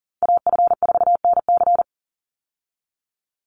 Morse code, heard as a keyed tone, about 40 wpm, 700 Hz. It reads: AF4NC
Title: Call Signs - 40wpm